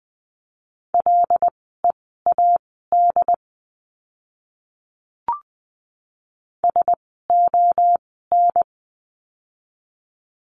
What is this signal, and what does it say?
Morse code, heard as a keyed tone, about 20 wpm, spaced as for 14 wpm, 700 Hz. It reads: LEAD SON
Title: 101 Lesson 17f - Letter u - Words 20wpm